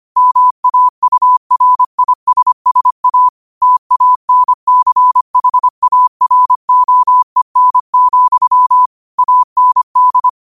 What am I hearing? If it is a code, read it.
Morse code, heard as a keyed tone, about 25 wpm, 1000 Hz. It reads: MAURISSA TANCHAROEN, AND
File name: qotdfiles / 25wpm